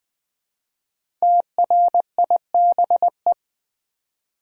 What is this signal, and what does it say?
Morse code, heard as a keyed tone, about 20 wpm, 700 Hz. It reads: TRIBE
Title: 5 Letter Words - First 3 Letters Given - 20wpm